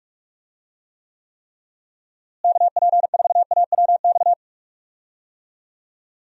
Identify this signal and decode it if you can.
Morse code, heard as a keyed tone, about 45 wpm, 700 Hz. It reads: KP4AWX
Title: DXCC-100 Call Signs - 6 Characters - Repeated 3x - 45wpm